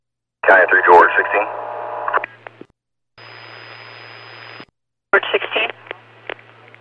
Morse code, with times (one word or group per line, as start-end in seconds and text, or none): none